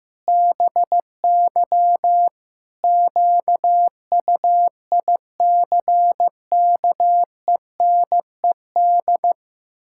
Morse code, heard as a keyed tone, about 15 wpm, 700 Hz. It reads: BY QUICKENED
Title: Pangram Challenge v2 - 15wpm